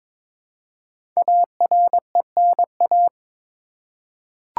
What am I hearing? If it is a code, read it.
Morse code, heard as a keyed tone, about 22 wpm, 700 Hz. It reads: ARENA
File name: 5 Letter Words - First 3 Letters Given - 22wpm